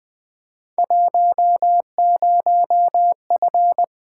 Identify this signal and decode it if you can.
Morse code, heard as a keyed tone, about 20 wpm, 700 Hz. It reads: 10F
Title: Field Day Exchanges Encore - 20wpm